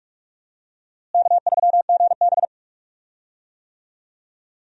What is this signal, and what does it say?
Morse code, heard as a keyed tone, about 46 wpm, 700 Hz. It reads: K3GB
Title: USA Call Signs - 4 Characters - Repeated 3x - 45wpm